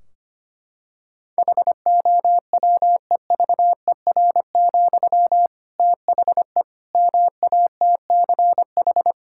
Morse code, heard as a keyed tone, about 25 wpm, 700 Hz. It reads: HOWEVER, THE MATCH
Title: Sentences from Top 1600 Words 25wpm